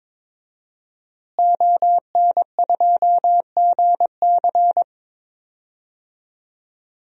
Morse code, heard as a keyed tone, about 22 wpm, 700 Hz. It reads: ON2GC
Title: DXCC-100 Call Signs - 5 Characters - Repeated 3x - 22wpm